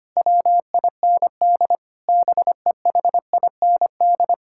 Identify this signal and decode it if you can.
Morse code, heard as a keyed tone, about 25 wpm, 700 Hz. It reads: WIND BEHIND
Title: Sentences from Top 1000 Words - Review Part 6 - 25wpm